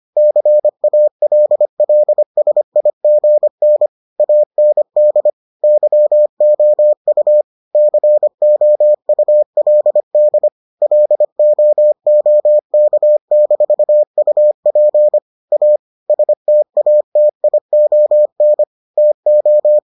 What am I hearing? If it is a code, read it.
Morse code, heard as a keyed tone, about 25 wpm, 600 Hz. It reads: CALLSIGN AND YOU COULD LOOK-UP A STATION TO